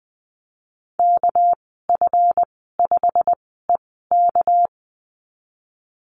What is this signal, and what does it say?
Morse code, heard as a keyed tone, about 20 wpm, spaced as for 15 wpm, 700 Hz. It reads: KF5EK
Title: 238 Lesson 38g - Letter q - Call Signs 20wpm